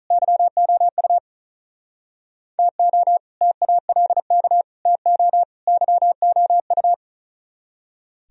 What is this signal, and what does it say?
Morse code, heard as a keyed tone, about 43 wpm, 700 Hz. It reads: YOU TO TALK TO YOU
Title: Speed-Racing - Sets of 4 Words 35wpm